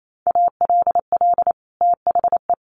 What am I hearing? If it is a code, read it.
Morse code, heard as a keyed tone, about 28 wpm, 700 Hz. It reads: ALL THE
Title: Sets of 2 Words 28wpm